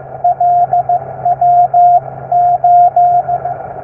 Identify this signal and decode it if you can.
Morse code, heard as a keyed tone, about 15 wpm, 710 Hz, with radio noise. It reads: LWO
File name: QSO-with-LW8DRU